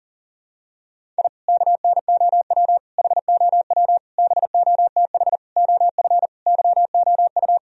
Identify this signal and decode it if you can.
Morse code, heard as a keyed tone, about 40 wpm, 700 Hz. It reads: I KNOW HOW BOTH OF YOU
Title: Sentences from Top 1000 Words - Review Part 1 - 40wpm